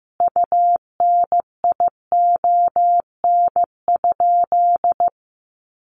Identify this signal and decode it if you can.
Morse code, heard as a keyed tone, about 15 wpm, 700 Hz. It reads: UNION?
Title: Sentences from Top 1400 Words 15wpm